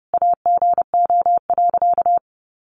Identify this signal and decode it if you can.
Morse code, heard as a keyed tone, about 30 wpm, 700 Hz. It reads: AGO.
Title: Sentences from Top 500 Words 30wpm